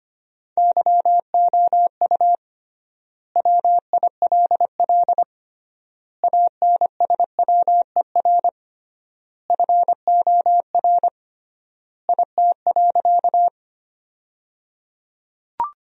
Morse code, heard as a keyed tone, about 25 wpm, 700 Hz. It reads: YOU WILL ANSWER FOR IT.
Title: 3x Word-Spacing - Sentences from Top 300 Words - 25wpm